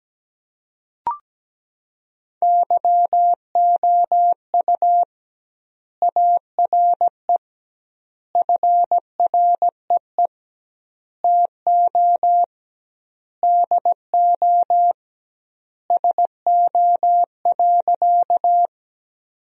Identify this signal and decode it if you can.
Morse code, heard as a keyed tone, about 17 wpm, 700 Hz. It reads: YOU ARE FREE TO DO SO.
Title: 2x Word-Spacing - Sentences from Top 500 Words - 17wpm